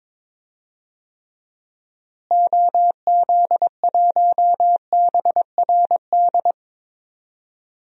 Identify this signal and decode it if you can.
Morse code, heard as a keyed tone, about 21 wpm, 700 Hz. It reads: OZ1BRD
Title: DXCC-100 Call Signs - 6 Characters - Repeated 3x - 22wpm